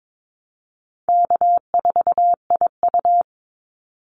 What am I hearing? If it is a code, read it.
Morse code, heard as a keyed tone, about 22 wpm, 700 Hz. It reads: K4IU
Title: Call Signs - 22wpm